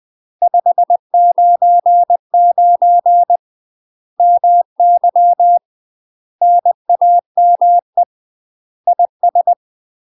Morse code, heard as a keyed tone, about 20 wpm, 700 Hz. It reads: 599 MY NAME IS
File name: UK MASTER QSOs TRAINING / 20WPM / UK MASTER QSOs 0058